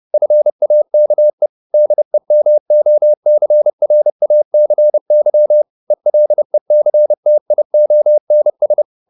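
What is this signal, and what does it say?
Morse code, heard as a keyed tone, about 30 wpm, 600 Hz. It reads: FAKE DEMOCRACY ELECTIONS
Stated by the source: Morse code at 30 wpm, the novel Never by Ken Follett in English, from CW med Gnister.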